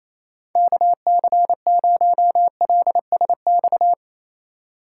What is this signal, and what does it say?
Morse code, heard as a keyed tone, about 28 wpm, 700 Hz. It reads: KC0LSX
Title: USA Call Signs - 6 Characters - 4x Character Spacing - 28wpm